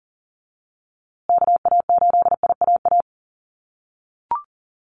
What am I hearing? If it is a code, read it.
Morse code, heard as a keyed tone, about 40 wpm, 700 Hz. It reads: KA8IAA
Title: USA Call Signs Repeated 3x 40wpm